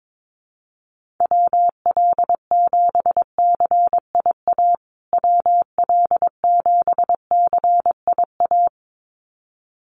Morse code, heard as a keyed tone, about 22 wpm, 700 Hz. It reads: WL7CIA WL7CIA
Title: USA Call Signs Repeated 2x 22wpm